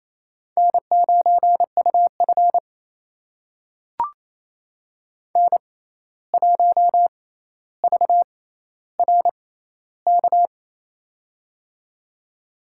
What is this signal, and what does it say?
Morse code, heard as a keyed tone, about 28 wpm, spaced as for 13 wpm, 700 Hz. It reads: N9UF N1VRK
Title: USA Call Signs - 6x Character Spacing - 28wpm